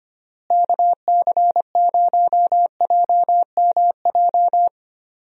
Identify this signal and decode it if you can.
Morse code, heard as a keyed tone, about 25 wpm, 700 Hz. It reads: KC0JMJ
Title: USA Call Signs - 6 Characters - Repeated 3x - 25wpm